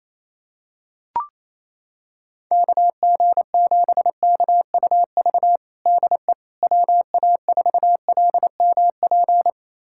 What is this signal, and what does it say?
Morse code, heard as a keyed tone, about 28 wpm, 700 Hz. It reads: KG7KUV DE WA4LMP
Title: USA Callsign DE Callsign - 6 Characters - 28wpm